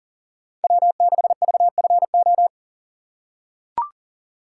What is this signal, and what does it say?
Morse code, heard as a keyed tone, about 40 wpm, 700 Hz. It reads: W6VFO
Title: USA Call Signs - 5 Characters - 3x Character Spacing - 40wpm